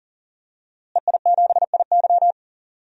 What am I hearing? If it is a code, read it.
Morse code, heard as a keyed tone, about 40 wpm, 700 Hz. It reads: EI7IY
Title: DXCC-100 Call Signs - 5 Characters - Repeated 3x - 40wpm